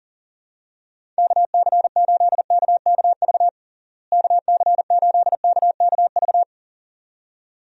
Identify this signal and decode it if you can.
Morse code, heard as a keyed tone, about 40 wpm, 700 Hz. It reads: KC8KKV KC8KKV
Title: USA Call Signs - 6 Characters - Repeated 2x - 40wpm